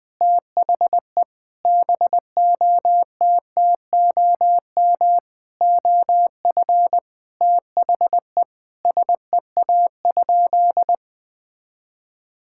Morse code, heard as a keyed tone, about 20 wpm, 700 Hz. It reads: THE BOTTOM OF THE SEA?
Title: Sentences from Top 1000 Words - Review Part 6 - 20wpm